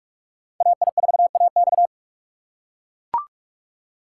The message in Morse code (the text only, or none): AI4AX